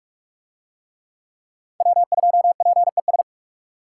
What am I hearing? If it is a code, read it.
Morse code, heard as a keyed tone, about 44 wpm, 700 Hz. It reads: W2PES